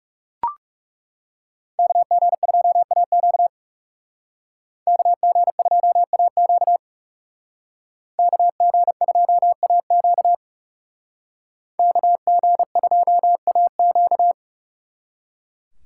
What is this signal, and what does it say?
Morse code, heard as a keyed tone, about 38 wpm, 700 Hz. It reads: KG2AQ KG2AQ KG2AQ KG2AQ